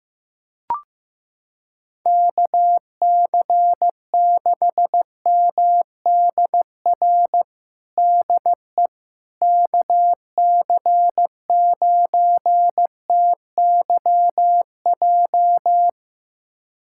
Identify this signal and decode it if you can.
Morse code, heard as a keyed tone, about 15 wpm, 700 Hz. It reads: KC6MDR DE KC9TYJ
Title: USA Callsign DE Callsign - 6 Characters - 15wpm